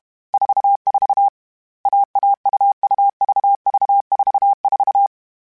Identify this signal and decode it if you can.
Morse code, heard as a keyed tone, about 32 wpm, 800 Hz. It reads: 44 AAUUVV44
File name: CW Code Groups-mp3s / Sets mp3s / AUV AUV4 sp @30